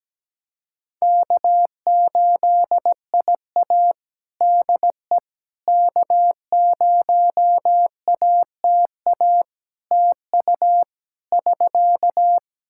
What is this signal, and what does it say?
Morse code, heard as a keyed tone, about 17 wpm, 700 Hz. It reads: K8IA DE K0ATA TU <SK>
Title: Practice QSO - Exchange 4 - Ending - 17wpm